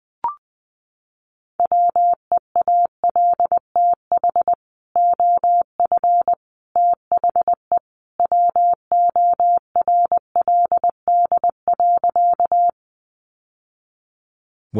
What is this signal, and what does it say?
Morse code, heard as a keyed tone, about 20 wpm, 700 Hz. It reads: WEALTH OF THE WORLD.
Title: Sentences from Top 1200 Words 20wpm